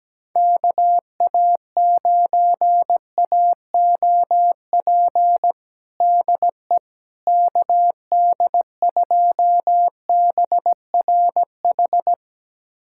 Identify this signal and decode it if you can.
Morse code, heard as a keyed tone, about 17 wpm, 700 Hz. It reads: KA9AOP DE KD2BRH
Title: USA Callsign DE Callsign - 6 Characters - 17wpm